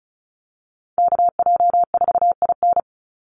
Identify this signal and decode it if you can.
Morse code, heard as a keyed tone, about 34 wpm, 700 Hz. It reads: KJ4IN
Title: USA Call Signs 35wpm